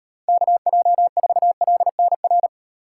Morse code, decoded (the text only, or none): KJ4LNR